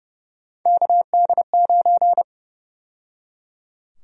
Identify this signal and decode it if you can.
Morse code, heard as a keyed tone, about 30 wpm, 700 Hz. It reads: KD9
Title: USA Call Sign Prefixes - 30wpm